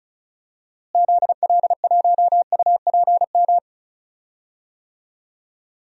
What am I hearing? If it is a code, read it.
Morse code, heard as a keyed tone, about 34 wpm, 700 Hz. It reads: ZL1UPM
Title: DXCC-100 Call Signs - 6 Characters - Repeated 3x - 35wpm